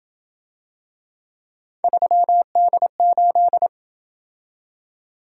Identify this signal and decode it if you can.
Morse code, heard as a keyed tone, about 27 wpm, 700 Hz. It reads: 3D8